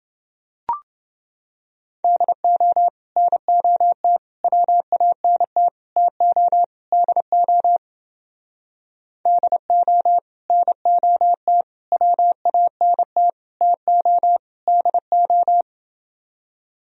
Morse code, heard as a keyed tone, about 29 wpm, 700 Hz. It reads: DO NOT WANT TO DO DO NOT WANT TO DO